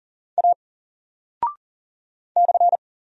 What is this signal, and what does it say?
Morse code, heard as a keyed tone, about 39 wpm, 700 Hz. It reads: A /
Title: Single Character (Letter:Number:Punctuation:Prosign) 40wpm